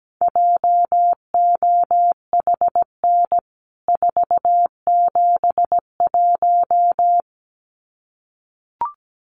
Morse code, heard as a keyed tone, about 17 wpm, 700 Hz. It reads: JOHN 471
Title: CWT Exchange 17wpm